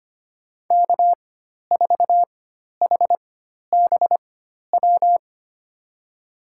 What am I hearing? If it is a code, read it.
Morse code, heard as a keyed tone, about 25 wpm, spaced as for 12 wpm, 700 Hz. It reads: K4HBW
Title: USA Call Signs - 5 Characters - 4x Character Spacing - 25wpm